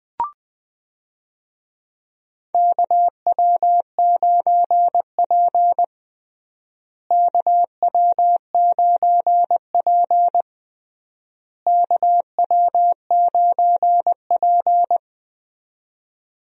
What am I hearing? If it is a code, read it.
Morse code, heard as a keyed tone, about 20 wpm, 700 Hz. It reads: KW9P KW9P KW9P
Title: USA Call Signs - 4 Characters - Repeated 3x - 20wpm